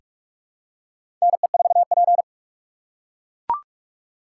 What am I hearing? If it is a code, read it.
Morse code, heard as a keyed tone, about 44 wpm, 700 Hz. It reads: NE4P
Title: USA Call Signs - 4 Characters - 6x Character Spacing - 45wpm